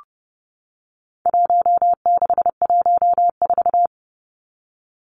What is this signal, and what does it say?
Morse code, heard as a keyed tone, about 30 wpm, 700 Hz. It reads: 1614
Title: Numbers - Contest Style - 4-Digit 30wpm